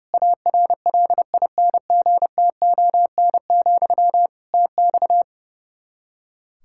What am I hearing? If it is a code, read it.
Morse code, heard as a keyed tone, about 30 wpm, 700 Hz. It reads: ARLINGTON, TX
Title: Practice QSO - Exchange 1 - Beginning - 30wpm